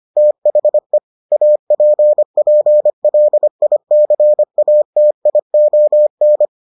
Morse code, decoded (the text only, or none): THE APPLICATION